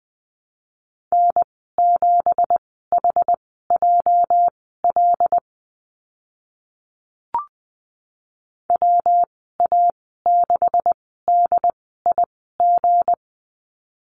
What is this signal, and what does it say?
Morse code, heard as a keyed tone, about 20 wpm, spaced as for 14 wpm, 700 Hz. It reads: N7HJL WA6DIG
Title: 213 Lesson 34g - Number 8 - Call Signs 20wpm